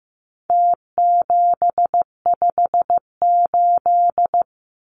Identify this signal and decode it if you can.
Morse code, heard as a keyed tone, about 15 wpm, 700 Hz. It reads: T758
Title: Numbers - Contest Style - 4-Digit 15wpm